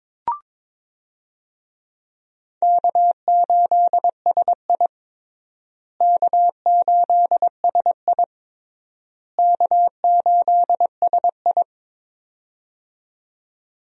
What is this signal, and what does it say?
Morse code, heard as a keyed tone, about 22 wpm, 700 Hz. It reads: K8SI K8SI K8SI
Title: USA Call Signs - 4 Characters - Repeated 3x - 22wpm